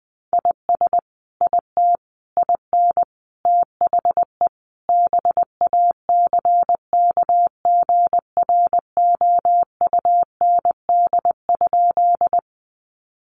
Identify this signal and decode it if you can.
Morse code, heard as a keyed tone, about 19 wpm, 700 Hz. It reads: IS IT IN THE BACKGROUND?
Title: Sentences from Top 1300 Words 20wpm